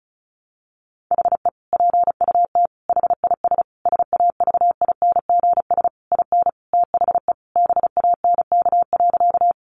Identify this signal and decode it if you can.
Morse code, heard as a keyed tone, about 34 wpm, 700 Hz. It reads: HE PUT HIS SAVINGS IN THE BANK.
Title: Sentences from Top 1200 Words 35wpm